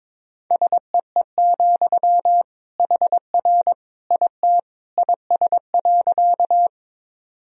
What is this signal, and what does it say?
Morse code, heard as a keyed tone, about 22 wpm, 700 Hz. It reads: SEE, HR IT IS.
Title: Sentences from Top 200 Words and Abbreviations 22wpm